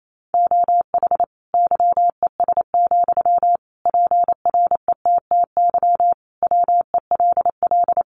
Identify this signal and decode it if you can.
Morse code, heard as a keyed tone, about 28 wpm, 700 Hz. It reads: OH YES, PRETTY WELL
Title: Sentences from Top 1000 Words 28wpm